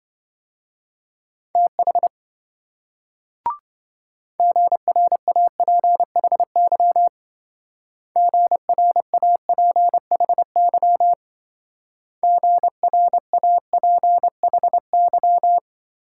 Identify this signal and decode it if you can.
Morse code, heard as a keyed tone, about 26 wpm, 700 Hz. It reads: TH GRAPHY GRAPHY GRAPHY